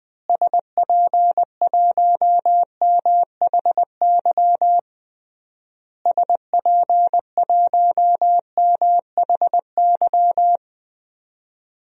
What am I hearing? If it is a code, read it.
Morse code, heard as a keyed tone, about 21 wpm, 700 Hz. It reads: SP1MHY SP1MHY
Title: DXCC-100 Call Signs - 6 Characters - Repeated 3x - 20wpm